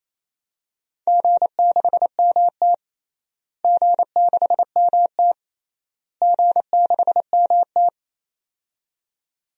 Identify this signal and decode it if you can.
Morse code, heard as a keyed tone, about 28 wpm, 700 Hz. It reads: G6MT G6MT G6MT
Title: DXCC-100 Call Signs - 4 Characters - Repeated 3x - 28wpm